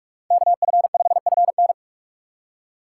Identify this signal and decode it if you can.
Morse code, heard as a keyed tone, about 45 wpm, 700 Hz. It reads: KF5FN